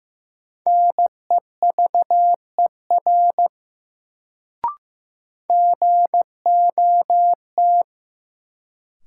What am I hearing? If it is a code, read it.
Morse code, heard as a keyed tone, about 15 wpm, 700 Hz. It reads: NEVER GOT
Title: Sentences from Top 300 Words - New Word Drill - 15wpm